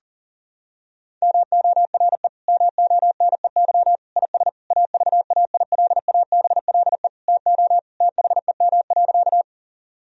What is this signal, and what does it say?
Morse code, heard as a keyed tone, about 40 wpm, 700 Hz. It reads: MORE MONEY IS AVAILABLE TO THEM.
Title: Sentences from Top 1000 Words - Review Part 3 - 40wpm